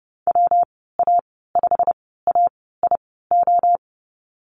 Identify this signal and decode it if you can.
Morse code, heard as a keyed tone, about 30 wpm, spaced as for 17 wpm, 700 Hz. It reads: WA5AIO
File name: Lesson 19g - Letter w - Call Signs 30wpm